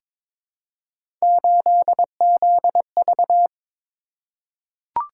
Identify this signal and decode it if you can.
Morse code, heard as a keyed tone, about 22 wpm, 700 Hz. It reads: 8ZV
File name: practice / 22wpm / Three Letter-Number Combinations - 22wpm